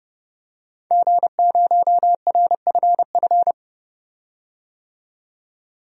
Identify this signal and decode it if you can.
Morse code, heard as a keyed tone, about 30 wpm, 700 Hz. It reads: G0RFF
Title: DXCC-100 Call Signs - 5 Characters - 30wpm